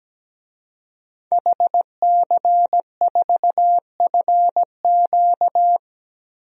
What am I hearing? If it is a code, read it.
Morse code, heard as a keyed tone, about 17 wpm, 700 Hz. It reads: HC4FQ
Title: DXCC-100 Call Signs - 5 Characters - 17wpm